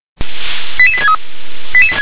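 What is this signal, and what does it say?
Five-tone selective call (ZVEI), heard as quick runs of five tones, tones of about 70 ms. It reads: 88863
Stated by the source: ZVEI five-tone selective call received on SDR.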